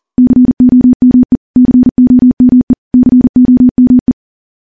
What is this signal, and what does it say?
Morse code, heard as a keyed tone, about 40 wpm, 250 Hz. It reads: COME COME COME
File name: morse / morse-training-40WPM